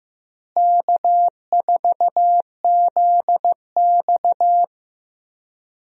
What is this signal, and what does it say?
Morse code, heard as a keyed tone, about 15 wpm, 700 Hz. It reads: K4ZX
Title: USA Call Signs - 4 Characters - Repeated 3x - 15wpm